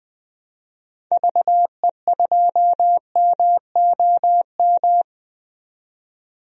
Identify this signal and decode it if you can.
Morse code, heard as a keyed tone, about 20 wpm, 700 Hz. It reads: VE2MOM
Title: DXCC-100 Call Signs - 6 Characters - 20wpm